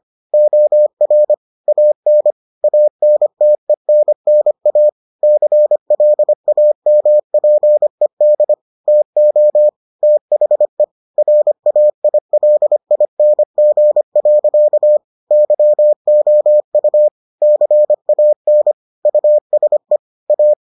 Morse code, heard as a keyed tone, about 25 wpm, 600 Hz. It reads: OR AN ANTENNA CLAMPED TO THE RAILING. YOU CAN USE A